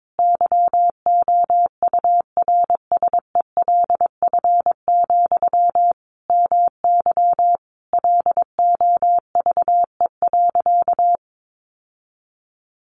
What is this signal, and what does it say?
Morse code, heard as a keyed tone, about 22 wpm, 700 Hz. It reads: YOURSELF, MY LOVE.
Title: Sentences from Top 1100 Words 22wpm